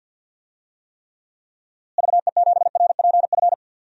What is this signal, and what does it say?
Morse code, heard as a keyed tone, about 46 wpm, 700 Hz. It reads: VE7RPF
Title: DXCC-100 Call Signs - 6 Characters - Repeated 2x - 50wpm